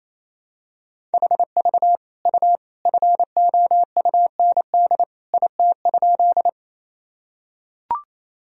Morse code, heard as a keyed tone, about 28 wpm, 700 Hz. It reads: HV U FOUND IT?
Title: Sentences from Top 300 Words and Abbreviations 28wpm